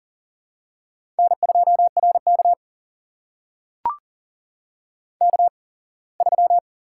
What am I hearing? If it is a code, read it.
Morse code, heard as a keyed tone, about 40 wpm, spaced as for 24 wpm, 700 Hz. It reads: N2RK K3
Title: USA Call Signs - 4 Characters - 8x Character Spacing - 40wpm